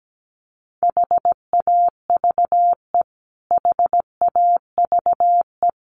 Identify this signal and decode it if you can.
Morse code, heard as a keyed tone, about 17 wpm, 700 Hz. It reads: HAVE HAVE
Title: Top 200 Words Repeated 17wpm